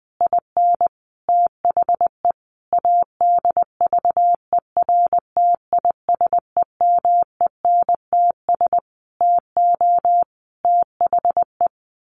Morse code, heard as a keyed tone, about 20 wpm, 700 Hz. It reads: IN THE ADVERTISEMENTS TO THE
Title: A Sound Of Thunder 20wpm